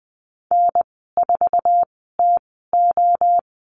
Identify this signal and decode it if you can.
Morse code, heard as a keyed tone, about 20 wpm, spaced as for 14 wpm, 700 Hz. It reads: N4TO
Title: 082 Lesson 14g - Number 2 - Call Signs 20wpm